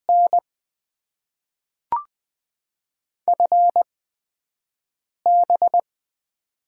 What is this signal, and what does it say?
Morse code, heard as a keyed tone, about 20 wpm, spaced as for 5 wpm, 700 Hz. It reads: N FB
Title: Top 20 QSO Elements - 8x Character Spacing - 20wpm